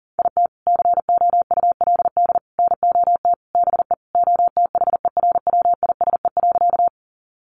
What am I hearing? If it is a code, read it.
Morse code, heard as a keyed tone, about 40 wpm, 700 Hz. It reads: IT COULD NOT BE OTHERWISE.
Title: Sentences from Top 1200 Words 40wpm